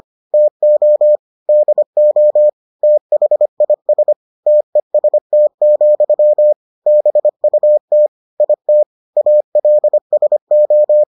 Morse code, heard as a keyed tone, about 25 wpm, 600 Hz. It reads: TO DO THIS TEST, BUT IT ALSO